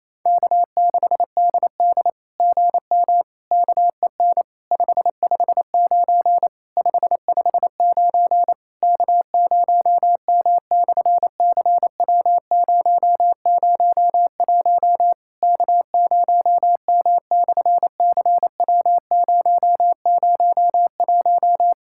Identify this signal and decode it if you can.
Morse code, heard as a keyed tone, about 28 wpm, 700 Hz. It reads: K6DD GM KEN 559 559 K0M/CW001 K0M/CW001
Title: SOTA QSO Practice 28wpm